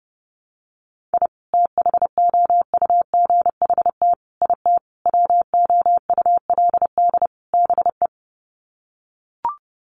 Morse code, heard as a keyed tone, about 30 wpm, 700 Hz. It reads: I THOUGHT IT WOULD BE
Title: Sets of 5 Words Encore 30wpm